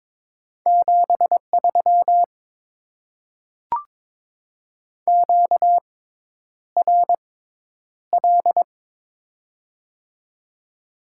Morse code, heard as a keyed tone, about 22 wpm, spaced as for 10 wpm, 700 Hz. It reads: 73 QRL
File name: Top 20 QSO Elements - 6x Character Spacing - 22wpm